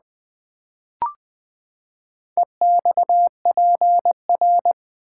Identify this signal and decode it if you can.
Morse code, heard as a keyed tone, about 20 wpm, 700 Hz. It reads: EXPR
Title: Build A Word - 4 Letters At A Time - 10-Letter Words - 20wpm